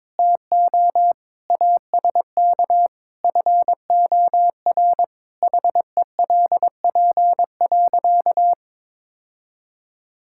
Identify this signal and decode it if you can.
Morse code, heard as a keyed tone, about 22 wpm, 700 Hz. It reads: TO ASK FOR HELP.
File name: practice / 22wpm / Sentences with 2-Word Phrases - 22wpm